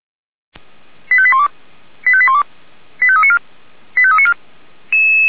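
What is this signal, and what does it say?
Five-tone selective call (ZVEI), heard as quick runs of five tones, tones of about 70 ms. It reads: 85812 85812 85394 85394
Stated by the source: ZVEI five-tone selective call 89123 received by SDR, from the Feuerwehr Orlamünde audio server.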